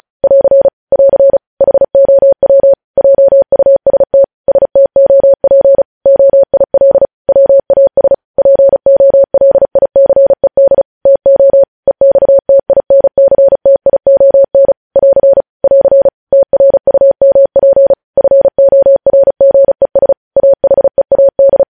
Morse code, heard as a keyed tone, about 36 wpm, 550 Hz. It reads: <AR> <AR> HOW JUST STOP OIL WAS POLICED TO EXTINCTION <AR> <AR> TRUMP FORGES AHEAD
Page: News headlines in Morse code at 35 wpm